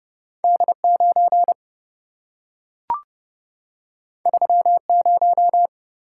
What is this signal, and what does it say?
Morse code, heard as a keyed tone, about 26 wpm, 700 Hz. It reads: D9 30